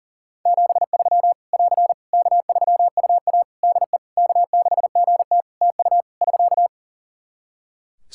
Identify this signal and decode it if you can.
Morse code, heard as a keyed tone, about 40 wpm, 700 Hz. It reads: 73 <AR> K3UA DE K6GT TU <SK>
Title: Practice QSO - Exchange 4 - Ending - 40wpm